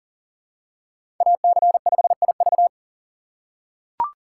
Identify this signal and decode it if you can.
Morse code, heard as a keyed tone, about 40 wpm, 700 Hz. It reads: AC5IV
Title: USA Call Signs - 5 Characters - 4x Character Spacing - 40wpm